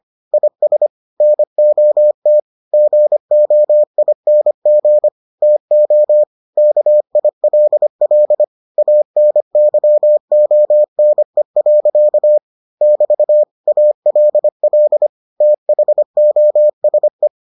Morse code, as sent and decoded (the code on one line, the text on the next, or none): .. ... / -. --- - / --. --- .. -. --. / - --- / -.- .. .-.. .-.. / .- -. -.-- --- -. .-.-.- / -...- / .- .-.. .-.. / - .... --- ... .
IS NOT GOING TO KILL ANYONE. <BT> ALL THOSE